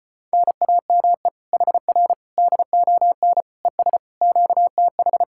Morse code, heard as a keyed tone, about 33 wpm, 700 Hz. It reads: NAME HR DON ES QTH